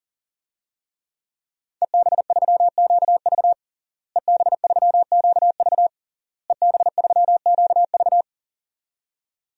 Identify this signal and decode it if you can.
Morse code, heard as a keyed tone, about 38 wpm, 700 Hz. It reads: EB3QV EB3QV EB3QV
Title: DXCC-100 Call Signs - 5 Characters - Repeated 3x - 40wpm